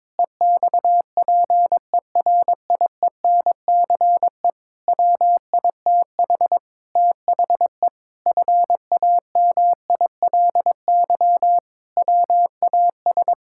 Morse code, heard as a keyed tone, about 22 wpm, 700 Hz. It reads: EXPERIENCE WITH THE FAMILY WAS
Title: Sentences from Top 1000 Words - Review Part 6 - 22wpm